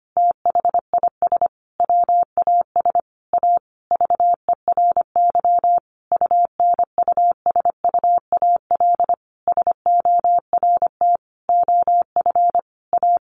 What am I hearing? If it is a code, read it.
Morse code, heard as a keyed tone, about 25 wpm, 700 Hz. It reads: THIS WAS A VERY UNUSUAL SORT OF A
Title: Sentences from Top 1400 Words 25wpm